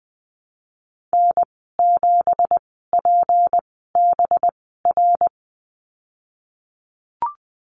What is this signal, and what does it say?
Morse code, heard as a keyed tone, about 20 wpm, spaced as for 15 wpm, 700 Hz. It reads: N7PBR
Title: 186 Lesson 30g - Letter b - Call Signs 20wpm